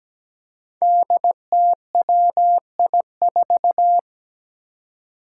Morse code, heard as a keyed tone, about 17 wpm, 700 Hz. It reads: DTWI4
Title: Five Challenging Letter-Number Combinations 17wpm